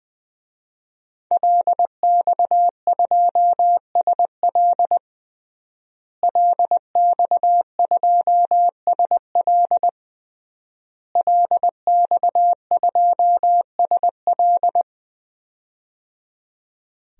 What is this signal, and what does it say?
Morse code, heard as a keyed tone, about 20 wpm, 700 Hz. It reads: LX2SL LX2SL LX2SL
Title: DXCC-100 Call Signs - 5 Characters - Repeated 3x - 20wpm